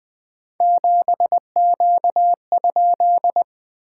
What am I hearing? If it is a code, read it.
Morse code, heard as a keyed tone, about 21 wpm, 690 Hz. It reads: 7Q?
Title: SOTA QSO Practice 20wpm